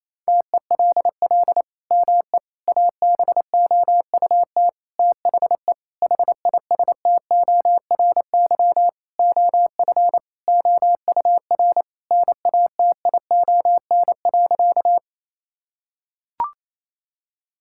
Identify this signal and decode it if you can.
Morse code, heard as a keyed tone, about 28 wpm, 700 Hz. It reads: TELL ME ABOUT THE HISTORY OF OUR NATION.
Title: Sentences from Top 1000 Words - Review Part 6 - 28wpm